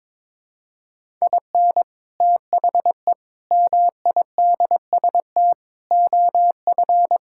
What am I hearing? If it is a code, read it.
Morse code, heard as a keyed tone, about 21 wpm, 700 Hz. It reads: IN THE MIDST OF
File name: practice / 22wpm / Sentences with 4-Word Phrases - 22wpm